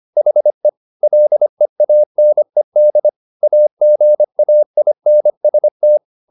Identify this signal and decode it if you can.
Morse code, heard as a keyed tone, about 25 wpm, 600 Hz. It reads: HE LEANED AGAINST
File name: the old man and the sea part 1 25wpm